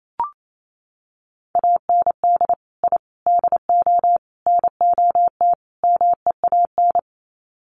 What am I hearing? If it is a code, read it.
Morse code, heard as a keyed tone, about 28 wpm, 700 Hz. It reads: AND I DO NOT MEAN
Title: Sets of 5 Words Encore2 28wpm